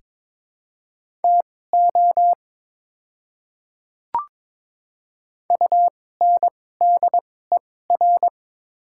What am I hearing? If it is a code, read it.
Morse code, heard as a keyed tone, about 23 wpm, spaced as for 16 wpm, 700 Hz. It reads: TO UNDER
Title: 2x Character Spacing - Top 300 Words - 22wpm